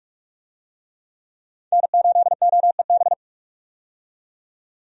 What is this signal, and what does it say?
Morse code, heard as a keyed tone, about 46 wpm, 700 Hz. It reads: N8OEB
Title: USA Call Signs - 5 Characters - 45wpm